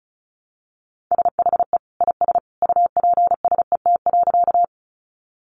Morse code, heard as a keyed tone, about 34 wpm, 700 Hz. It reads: SHE IS UPSET.
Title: Sentences from Top 800 Words 35wpm